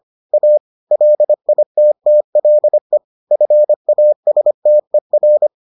Morse code, heard as a keyed tone, about 25 wpm, 600 Hz. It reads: A LITTLE FASTER